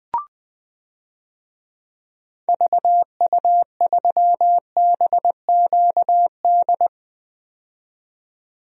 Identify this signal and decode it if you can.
Morse code, heard as a keyed tone, about 20 wpm, 700 Hz. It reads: VU3BQD
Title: DXCC-100 Call Signs - 6 Characters - 20wpm